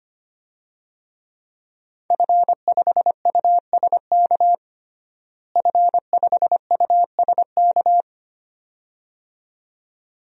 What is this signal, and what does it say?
Morse code, heard as a keyed tone, about 25 wpm, 700 Hz. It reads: F5USK F5USK
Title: DXCC-100 Call Signs - 5 Characters - Repeated 2x - 25wpm